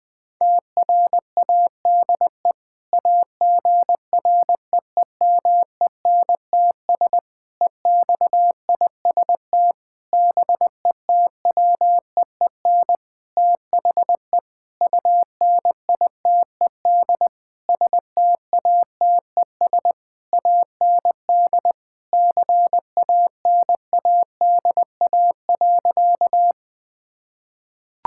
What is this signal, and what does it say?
Morse code, heard as a keyed tone, about 20 wpm, 700 Hz. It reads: TRADE AGREEMENTS EXIST BETWEEN THE UNITED STATES AND CANADA.
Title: Sentences with 5-Word Phrases - 20wpm